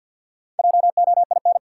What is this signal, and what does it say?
Morse code, heard as a keyed tone, about 52 wpm, 700 Hz. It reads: JOIN
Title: Sentences from Top 1000 Words - Review Part 6 - New Word Drill - 50wpm